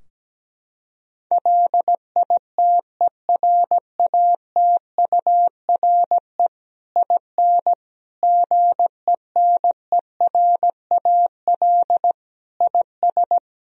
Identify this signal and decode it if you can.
Morse code, heard as a keyed tone, about 17 wpm, 700 Hz. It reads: LITERATURE IN GENERAL IS
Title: Sentences from Top 1000 Words - Review Part 3 - 17wpm